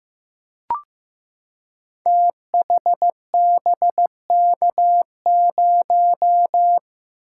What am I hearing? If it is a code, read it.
Morse code, heard as a keyed tone, about 15 wpm, 700 Hz. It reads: THBK0
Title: Code Groups - 5 Letter:Numbers - 15wpm